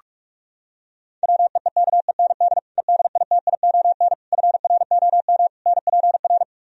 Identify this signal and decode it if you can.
Morse code, heard as a keyed tone, about 48 wpm, 700 Hz. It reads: WEEKEND EDITION FROM NPR